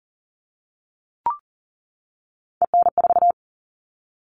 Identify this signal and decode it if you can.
Morse code, heard as a keyed tone, about 40 wpm, 700 Hz. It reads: EN4
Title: All Cut Numbers - 3-Digit 40wpm